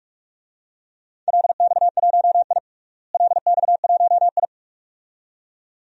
Morse code, heard as a keyed tone, about 46 wpm, 700 Hz. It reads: LX1I LX1I
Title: DXCC-100 Call Signs - 4 Characters - Repeated 2x - 45wpm